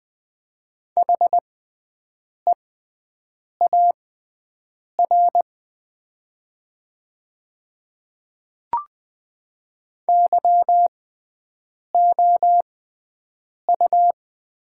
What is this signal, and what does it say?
Morse code, heard as a keyed tone, about 20 wpm, spaced as for 7 wpm, 700 Hz. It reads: HEAR YOU
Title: 6x Character Spacing - Top 500 Words - 20wpm